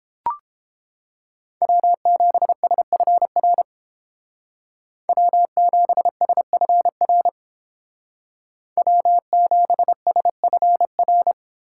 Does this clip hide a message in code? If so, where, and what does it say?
Yes: Morse code from 1.6 to 11.3 s, heard as a keyed tone, about 28 wpm, 700 Hz. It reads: W7SFR W7SFR W7SFR